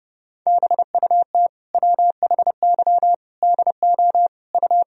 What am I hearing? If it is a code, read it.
Morse code, heard as a keyed tone, about 30 wpm, 700 Hz. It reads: BUT WHY DO U
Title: Sentences from Top 200 Words and Abbreviations 30wpm